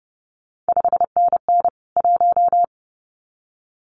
Morse code, heard as a keyed tone, about 30 wpm, 700 Hz. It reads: 5NN 1
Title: CQ WW DX Contest Exchange Encore - 30wpm